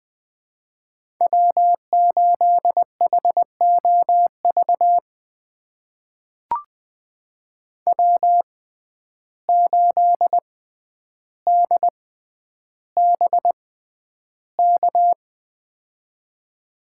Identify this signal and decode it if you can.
Morse code, heard as a keyed tone, about 20 wpm, spaced as for 10 wpm, 700 Hz. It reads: W8HOV W8DBK
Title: USA Call Signs - 6x Character Spacing - 20wpm